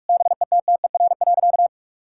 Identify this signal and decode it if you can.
Morse code, heard as a keyed tone, about 41 wpm, 700 Hz. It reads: BETTER.
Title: Sentences from Top 1600 Words 45wpm